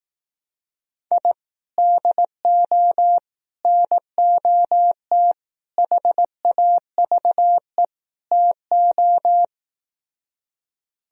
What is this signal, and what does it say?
Morse code, heard as a keyed tone, about 18 wpm, 700 Hz. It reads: I DO NOT HAVE TO
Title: Speed-Racing - Sets of 5 Words 15wpm